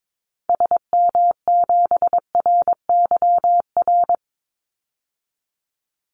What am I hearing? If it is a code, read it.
Morse code, heard as a keyed tone, about 22 wpm, 700 Hz. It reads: SM7RYR
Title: Call Signs Repeated 2x - 22wpm